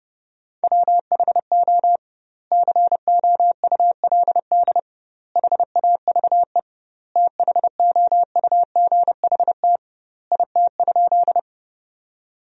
Morse code, heard as a keyed tone, about 30 wpm, 700 Hz. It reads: WHO COULD HAVE THOUGHT IT?
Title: 2x Word-Spacing - Sentences from Top 300 Words - 30wpm